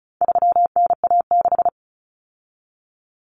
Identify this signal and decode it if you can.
Morse code, heard as a keyed tone, about 34 wpm, 700 Hz. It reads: 3NA6
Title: Common Cut Numbers - 4-Digit 35wpm